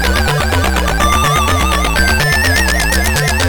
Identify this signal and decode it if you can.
There is no code.